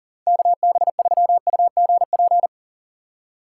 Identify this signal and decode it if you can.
Morse code, heard as a keyed tone, about 39 wpm, 700 Hz. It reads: KB3UGP
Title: USA Call Signs - 6 Characters - 3x Character Spacing - 40wpm